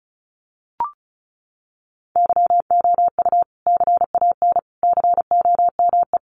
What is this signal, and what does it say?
Morse code, heard as a keyed tone, about 34 wpm, 700 Hz. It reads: YOU CAN COME
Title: Sentences from Top 1200 Words 35wpm